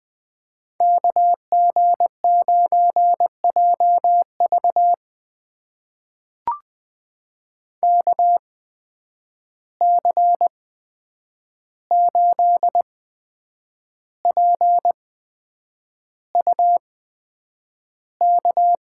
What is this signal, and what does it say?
Morse code, heard as a keyed tone, about 20 wpm, spaced as for 8 wpm, 700 Hz. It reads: KG9JV KC8PUK
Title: USA Call Signs - 8x Character Spacing - 20wpm